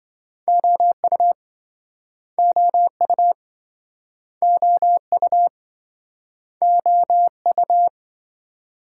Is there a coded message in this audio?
Yes: Morse code from 0.5 to 7.9 s, heard as a keyed tone, about 28 wpm, 700 Hz. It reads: OU OU OU OU